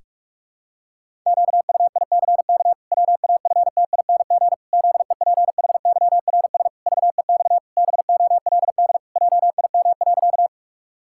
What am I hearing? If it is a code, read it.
Morse code, heard as a keyed tone, about 46 wpm, 700 Hz. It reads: QUICK WAFTING ZEPHYRS VEX BOLD JIM.
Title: Pangram Challenge v2 - 45wpm